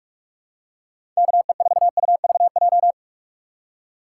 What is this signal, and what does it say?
Morse code, heard as a keyed tone, about 46 wpm, 700 Hz. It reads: KE4UVJ
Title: USA Call Signs - 6 Characters - 3x Character Spacing - 45wpm